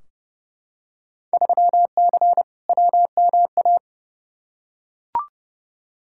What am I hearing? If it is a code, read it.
Morse code, heard as a keyed tone, about 30 wpm, 700 Hz. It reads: 3C WMA
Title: Field Day Exchanges Encore - 30wpm